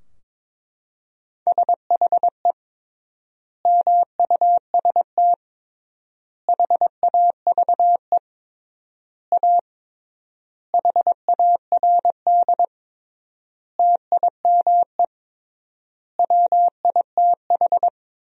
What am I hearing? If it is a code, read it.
Morse code, heard as a keyed tone, about 22 wpm, 700 Hz. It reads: SHE MUST HAVE A HARD TIME WITH
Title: 3x Word-Spacing - Sentences from Top 300 Words - 22wpm